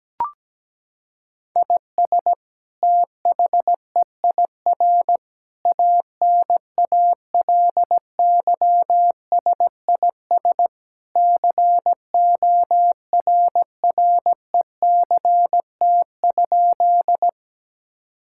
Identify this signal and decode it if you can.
Morse code, heard as a keyed tone, about 17 wpm, 700 Hz. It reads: IS THEIR ANALYSIS CORRECT?
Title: Sentences from Top 1000 Words - Review Part 5 - 17wpm